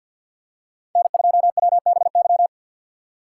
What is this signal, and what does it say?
Morse code, heard as a keyed tone, about 51 wpm, 700 Hz. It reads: N2WBY
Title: USA Call Signs - 5 Characters - 4x Character Spacing - 50wpm